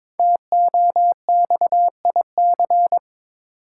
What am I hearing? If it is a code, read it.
Morse code, heard as a keyed tone, about 22 wpm, 690 Hz. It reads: TOXIC
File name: 5 Letter Words - First 3 Letters Given - 22wpm